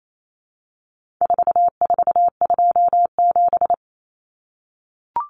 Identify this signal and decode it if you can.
Morse code, heard as a keyed tone, about 28 wpm, 700 Hz. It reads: 4427
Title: Numbers - 4-Digit 28wpm